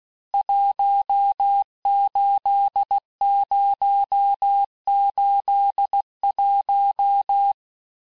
Morse code, heard as a keyed tone, about 16 wpm, 800 Hz. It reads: 18081
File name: cours-du-lundi files / cours-1592--teste-FIN-de-QSO-vitesse-15mpm